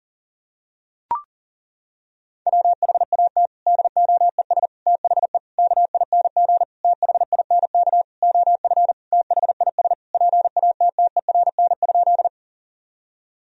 Morse code, heard as a keyed tone, about 40 wpm, 700 Hz. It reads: WHAT DOES THE KING THINK OF THIS PATTERN?
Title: Sentences from Top 1000 Words - Review Part 2 - 40wpm